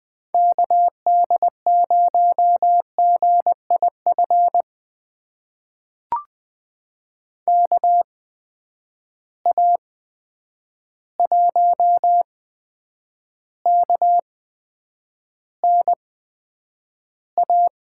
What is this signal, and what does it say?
Morse code, heard as a keyed tone, about 20 wpm, spaced as for 9 wpm, 700 Hz. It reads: KD0GIF KA1KNA
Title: USA Call Signs - 6 Characters - 8x Character Spacing - 20wpm